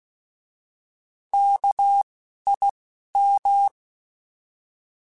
Morse code, heard as a keyed tone, about 16 wpm, 800 Hz. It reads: K I M
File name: Alphabetical / Alphabetical mp3s / Ks @10